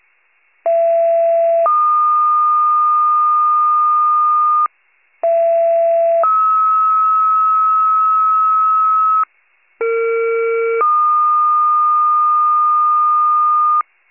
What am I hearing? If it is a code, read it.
Two-tone page, heard as a one-second tone, then a three-second tone, about 670 then 1220 Hz, about 670 then 1290 Hz, about 460 then 1220 Hz. It is heard from 0.7 to 13.8 s.